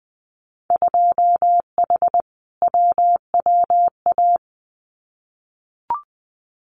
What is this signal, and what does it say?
Morse code, heard as a keyed tone, about 20 wpm, 700 Hz. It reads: 2H WWA
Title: Winter Field Day Exchanges 20wpm